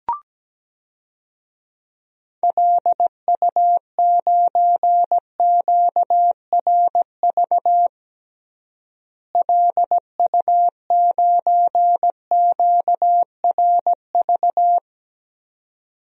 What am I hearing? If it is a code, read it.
Morse code, heard as a keyed tone, about 17 wpm, 700 Hz. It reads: LU9QRV LU9QRV
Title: DXCC-100 Call Signs - 6 Characters - Repeated 2x - 17wpm